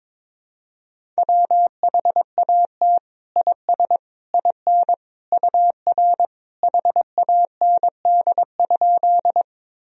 Morse code, heard as a keyed tone, about 22 wpm, 700 Hz. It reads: WHAT IS IN UR HAND?
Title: Sentences from Top 200 Words and Abbreviations 22wpm